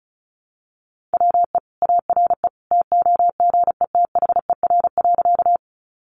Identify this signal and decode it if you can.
Morse code, heard as a keyed tone, about 34 wpm, 700 Hz. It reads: WE ARE TOGETHER.
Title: Sentences from Top 300 Words 35wpm